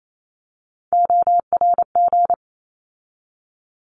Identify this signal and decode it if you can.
Morse code, heard as a keyed tone, about 28 wpm, 700 Hz. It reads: ORG
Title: ARRL Sections 28wpm